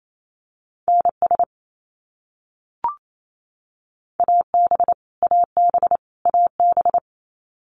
Morse code, heard as a keyed tone, about 28 wpm, 700 Hz. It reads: NS AB AB AB
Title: Canadian Provinces and Territories - Repeated 3x - 28wpm